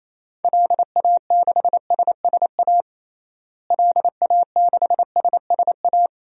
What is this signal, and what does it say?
Morse code, heard as a keyed tone, about 28 wpm, 700 Hz. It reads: LA6SSA LA6SSA
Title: DXCC-100 Call Signs - 6 Characters - Repeated 2x - 28wpm